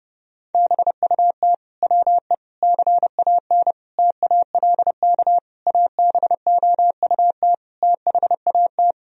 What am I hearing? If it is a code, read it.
Morse code, heard as a keyed tone, about 30 wpm, 700 Hz. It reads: BUT WE CAN TALK ABOUT THAT
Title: Sentences from Top 1000 Words - Review Part 4 - 30wpm